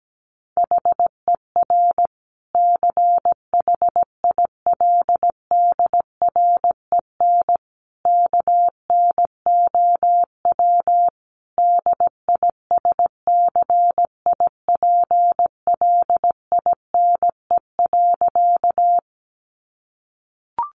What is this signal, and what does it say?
Morse code, heard as a keyed tone, about 17 wpm, 700 Hz. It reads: HER CHILDREN KNOW DISCIPLINE.
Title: Sentences from Top 900 Words 17wpm